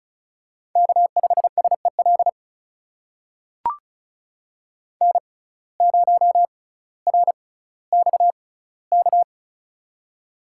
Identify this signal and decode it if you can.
Morse code, heard as a keyed tone, about 34 wpm, spaced as for 18 wpm, 700 Hz. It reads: K5SEL N0RXK
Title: USA Call Signs - 6x Character Spacing - 35wpm